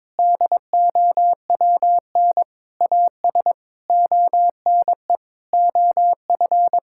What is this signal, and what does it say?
Morse code, heard as a keyed tone, about 22 wpm, 700 Hz. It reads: DOWN AS ONE OF
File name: practice / 22wpm / Sets of 7 Words - 22wpm